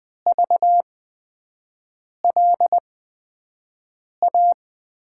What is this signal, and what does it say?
Morse code, heard as a keyed tone, about 20 wpm, spaced as for 5 wpm, 700 Hz. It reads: VLA
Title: USA Call Signs - 6 Characters - 8x Character Spacing - 20wpm